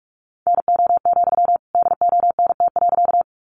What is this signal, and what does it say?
Morse code, heard as a keyed tone, about 44 wpm, 700 Hz. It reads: NO, DONT.
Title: Sentences from Top 300 Words 45wpm